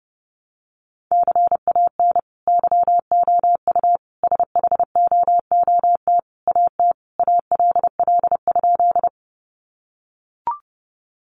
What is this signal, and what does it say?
Morse code, heard as a keyed tone, about 30 wpm, 700 Hz. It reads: CAN YOU SHOOT AT ALL?
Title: Sentences from Top 1200 Words 30wpm